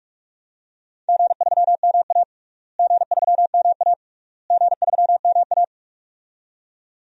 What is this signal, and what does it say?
Morse code, heard as a keyed tone, about 46 wpm, 700 Hz. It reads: G3MA G3MA G3MA
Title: DXCC-100 Call Signs - 4 Characters - Repeated 3x - 45wpm